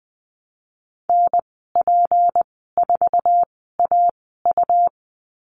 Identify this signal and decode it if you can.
Morse code, heard as a keyed tone, about 20 wpm, spaced as for 15 wpm, 700 Hz. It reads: NP4AU
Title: 179 Lesson 29g - Slash - Call Signs 20wpm